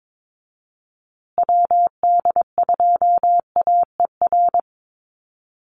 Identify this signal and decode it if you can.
Morse code, heard as a keyed tone, about 22 wpm, 700 Hz. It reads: WD2AER
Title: USA Call Signs Repeated 3x 22wpm